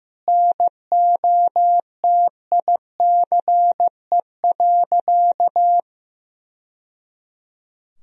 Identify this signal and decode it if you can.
Morse code, heard as a keyed tone, about 16 wpm, 700 Hz. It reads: NOTICE.
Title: Sentences from Top 1000 Words - Review Part 2 - 15wpm